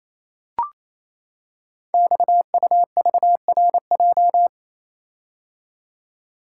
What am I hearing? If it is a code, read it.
Morse code, heard as a keyed tone, about 29 wpm, 700 Hz. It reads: XUVRJ
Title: Code Groups - 5 Letters - 28wpm